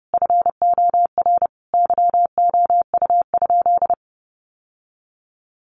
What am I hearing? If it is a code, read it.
Morse code, heard as a keyed tone, about 28 wpm, 700 Hz. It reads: FOR YOU?
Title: Sentences from Top 1400 Words 30wpm